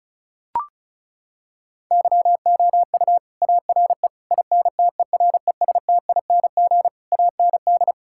Morse code, heard as a keyed tone, about 36 wpm, 700 Hz. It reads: YOU ARE INTERESTING AND
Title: Sentences from Top 1000 Words - Review Part 6 - 35wpm